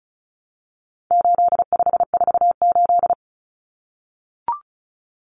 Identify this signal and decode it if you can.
Morse code, heard as a keyed tone, about 34 wpm, 700 Hz. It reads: 8548
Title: Numbers - 4-Digit 35wpm